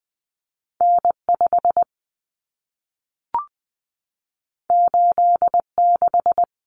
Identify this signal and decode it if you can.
Morse code, heard as a keyed tone, about 20 wpm, 700 Hz. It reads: N5 86
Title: Common Cut Numbers - 2-Digit 20wpm